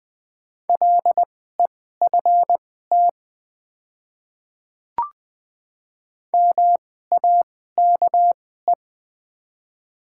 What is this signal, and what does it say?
Morse code, heard as a keyed tone, about 20 wpm, spaced as for 14 wpm, 700 Hz. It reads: LEFT MAKE
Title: 2x Character Spacing - Top 300 Words - 20wpm